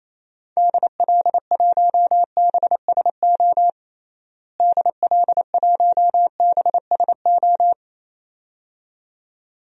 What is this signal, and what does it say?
Morse code, heard as a keyed tone, about 28 wpm, 700 Hz. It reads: DL1BSO DL1BSO
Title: DXCC-100 Call Signs - 6 Characters - Repeated 2x - 28wpm